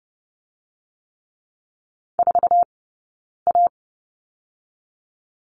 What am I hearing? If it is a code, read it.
Morse code, heard as a keyed tone, about 31 wpm, 700 Hz. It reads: 4 A
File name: Lesson 17d - Letter u - Two Characters 30wpm